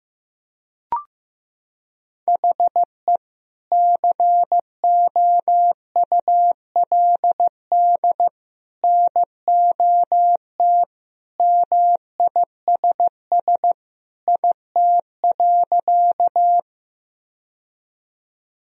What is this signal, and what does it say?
Morse code, heard as a keyed tone, about 15 wpm, 700 Hz. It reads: HE COULD NOT MISS IT.
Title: Sentences from Top 1000 Words - Review Part 3 - 15wpm